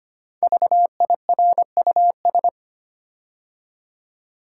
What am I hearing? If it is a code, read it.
Morse code, heard as a keyed tone, about 25 wpm, 700 Hz. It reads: VIRUS
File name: 5 Letter Words - First 3 Letters Given - 25wpm